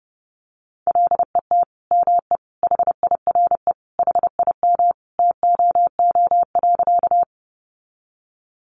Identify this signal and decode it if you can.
Morse code, heard as a keyed tone, about 30 wpm, 700 Hz. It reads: LET ME HIRE HIM TOO.
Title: Sentences from Top 1400 Words 30wpm